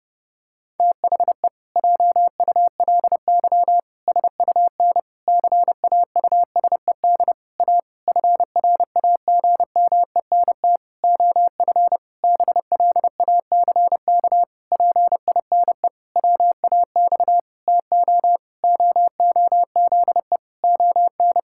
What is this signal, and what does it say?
Morse code, heard as a keyed tone, about 32 wpm, 700 Hz. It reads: THE JULY SUN CAUSED A FRAGMENT OF BLACK PINE WAX TO OOZE ON
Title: Pangram Challenge v2 - 30wpm